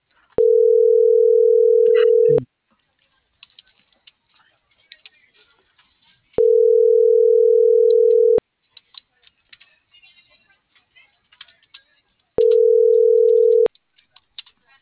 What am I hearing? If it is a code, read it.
Ringback tone (North America), 440 + 480 Hz.